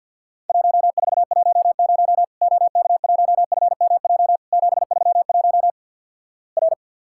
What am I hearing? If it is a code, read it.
Morse code, heard as a keyed tone, about 51 wpm, 700 Hz. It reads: 1410 OK1FMJ 731 R
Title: Worked All Europe DX Contest - QTC Exchange - 50wpm